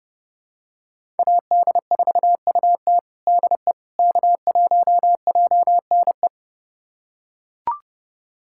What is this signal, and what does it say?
Morse code, heard as a keyed tone, about 30 wpm, 700 Hz. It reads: AD4UT DE K1JNE
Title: USA Callsign DE Callsign - 5 Characters - 30wpm